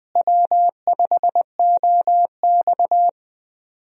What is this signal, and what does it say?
Morse code, heard as a keyed tone, about 20 wpm, 700 Hz. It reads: W5OX
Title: USA Call Signs - 4 Characters - Repeated 2x - 20wpm